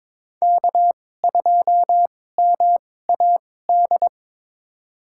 Morse code, heard as a keyed tone, about 22 wpm, spaced as for 16 wpm, 700 Hz. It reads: K2MAD
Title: USA Call Signs - 5 Characters - 2x Character Spacing - 22wpm